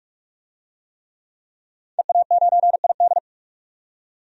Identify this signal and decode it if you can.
Morse code, heard as a keyed tone, about 44 wpm, 700 Hz. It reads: EA9ID
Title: DXCC-100 Call Signs - 5 Characters - 45wpm